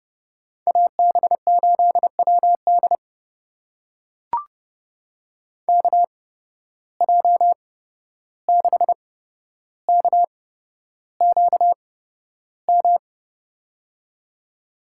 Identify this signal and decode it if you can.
Morse code, heard as a keyed tone, about 30 wpm, spaced as for 12 wpm, 700 Hz. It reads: AB8WD KJ6KQM
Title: USA Call Signs - 8x Character Spacing - 30wpm